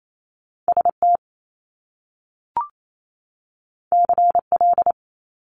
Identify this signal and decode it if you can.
Morse code, heard as a keyed tone, about 28 wpm, 700 Hz. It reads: ST CL
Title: Top Two Letter Combinations in Words 28wpm